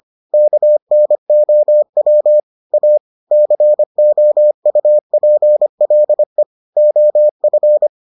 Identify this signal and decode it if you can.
Morse code, heard as a keyed tone, about 25 wpm, 600 Hz. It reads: KNOW A COUPLE OF